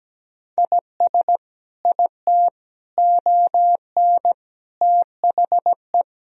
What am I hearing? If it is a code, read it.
Morse code, heard as a keyed tone, about 17 wpm, 700 Hz. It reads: IS IT ON THE
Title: Sentences from Top 1000 Words - Review Part 4 - 17wpm